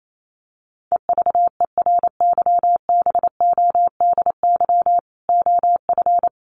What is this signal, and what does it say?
Morse code, heard as a keyed tone, about 28 wpm, 700 Hz. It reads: EVERYBODY OF
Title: Sentences from Top 1400 Words 28wpm